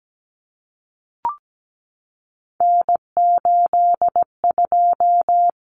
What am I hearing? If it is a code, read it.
Morse code, heard as a keyed tone, about 17 wpm, 700 Hz. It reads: N82
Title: Common Cut Numbers - 3-Digit 17wpm